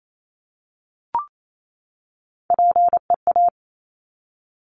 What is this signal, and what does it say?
Morse code, heard as a keyed tone, about 28 wpm, 700 Hz. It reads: PEA
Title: Top Three Letter Combinations in Words 28wpm